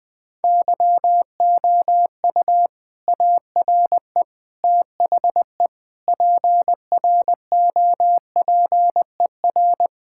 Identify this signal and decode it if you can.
Morse code, heard as a keyed tone, about 20 wpm, 700 Hz. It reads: YOU ARE THE PROPER
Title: Sentences from Top 1000 Words - Review Part 5 - 20wpm